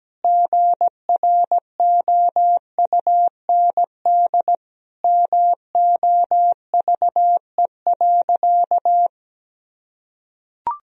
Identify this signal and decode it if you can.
Morse code, heard as a keyed tone, about 17 wpm, 700 Hz. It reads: GROUND MOVE.
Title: Sentences from Top 1000 Words - Review Part 2 - 17wpm